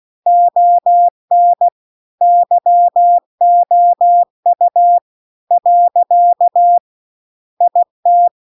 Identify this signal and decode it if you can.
Morse code, heard as a keyed tone, about 16 wpm, 700 Hz. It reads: ON YOU . IT